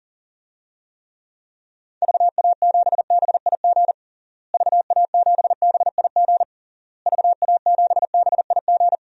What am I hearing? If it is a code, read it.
Morse code, heard as a keyed tone, about 40 wpm, 700 Hz. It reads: VA7BIG VA7BIG VA7BIG
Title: DXCC-100 Call Signs - 6 Characters - Repeated 3x - 40wpm